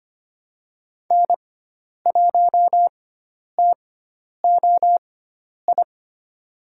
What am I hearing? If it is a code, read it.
Morse code, heard as a keyed tone, about 25 wpm, spaced as for 10 wpm, 700 Hz. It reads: N1TOI
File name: Lesson 16g - Letter c - Call Signs 25wpm